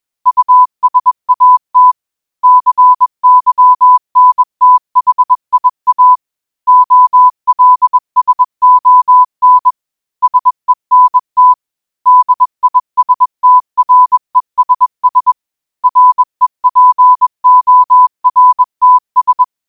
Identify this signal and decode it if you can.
Morse code, heard as a keyed tone, about 21 wpm, 1000 Hz. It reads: USAT CYNTHIA OLSON SENT DISTRESS REPORTS